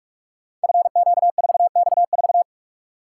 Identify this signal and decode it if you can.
Morse code, heard as a keyed tone, about 44 wpm, 700 Hz. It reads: FQ4X4
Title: Code Groups - 5 Letter:Numbers - 45wpm